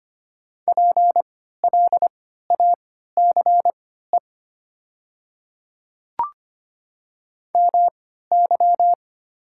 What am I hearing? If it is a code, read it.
Morse code, heard as a keyed tone, about 25 wpm, spaced as for 14 wpm, 700 Hz. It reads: PLACE MY
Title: 3x Character Spacing - Top 200 Words - 25wpm